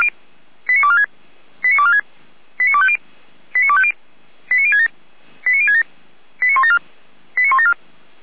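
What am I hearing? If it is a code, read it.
Five-tone selective call (ZVEI), heard as quick runs of five tones, tones of about 80 ms. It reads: 89246 89246 89250 89250 89067 89067 89174 89174
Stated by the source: ZVEI five-tone call received by SDR, Feuerwehr Orlamünde.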